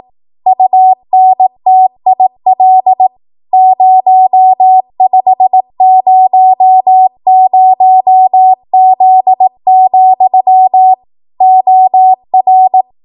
Morse code, heard as a keyed tone, about 18 wpm, 750 Hz. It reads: UNTIL 0500Z, OR
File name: Archive / 18 WPM / 18WPM